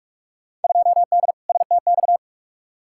Morse code, heard as a keyed tone, about 46 wpm, 700 Hz. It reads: 2D STX
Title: Field Day Exchanges - 45wpm